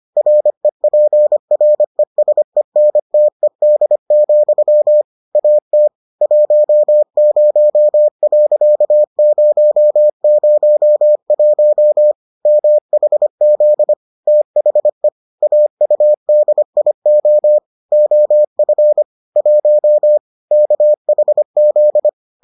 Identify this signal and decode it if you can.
Morse code, heard as a keyed tone, about 25 wpm, 600 Hz. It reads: REPRESENTED, AT 10.001 MHZ THE AUDIO OF 1 KHZ